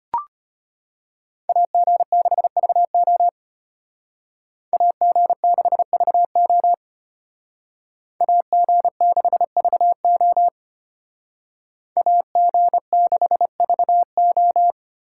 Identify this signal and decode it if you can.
Morse code, heard as a keyed tone, about 32 wpm, 700 Hz. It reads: AG6VO AG6VO AG6VO AG6VO